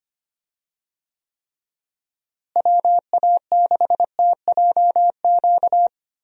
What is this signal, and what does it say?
Morse code, heard as a keyed tone, about 25 wpm, 700 Hz. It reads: WA6TJQ
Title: USA Call Signs - 6 Characters - Repeated 3x - 25wpm